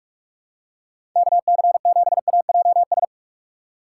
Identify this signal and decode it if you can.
Morse code, heard as a keyed tone, about 46 wpm, 700 Hz. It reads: KC7AJS
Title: USA Call Signs - 6 Characters - 6x Character Spacing - 45wpm